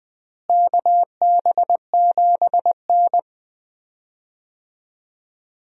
Morse code, heard as a keyed tone, about 20 wpm, 700 Hz. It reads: KB7N
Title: USA Call Signs - 4 Characters - Repeated 3x - 20wpm